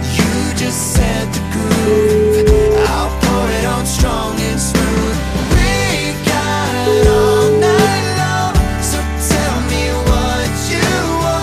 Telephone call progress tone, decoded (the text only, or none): ringback tone